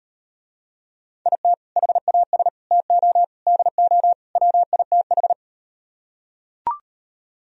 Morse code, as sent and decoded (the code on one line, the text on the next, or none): .. - / .... .- ... / - --- / -.. --- / .-- .. - ....
IT HAS TO DO WITH